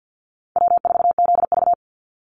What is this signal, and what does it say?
Morse code, heard as a keyed tone, about 50 wpm, 700 Hz. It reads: W4ZV
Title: NA SOTA Chaser Callsigns 50wpm